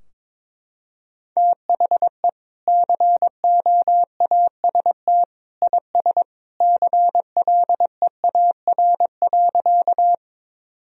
Morse code, heard as a keyed tone, about 22 wpm, 700 Hz. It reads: THE COAST IS CLEAR.
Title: Sentences from Top 1000 Words - Review Part 4 - 22wpm